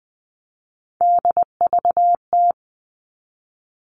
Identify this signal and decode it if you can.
Morse code, heard as a keyed tone, about 20 wpm, 700 Hz. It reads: DVT
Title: All Cut Numbers - 3-Digit 20wpm